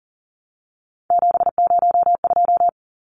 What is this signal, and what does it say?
Morse code, heard as a keyed tone, about 40 wpm, 700 Hz. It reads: 702
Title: Numbers - 3-Digit 40wpm